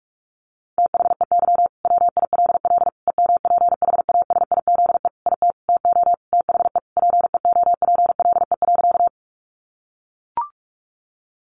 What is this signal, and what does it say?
Morse code, heard as a keyed tone, about 44 wpm, 700 Hz. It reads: THEY WILL EMPHASIZE IT TO THE PEOPLE.
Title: Sentences from Top 1500 Words 45wpm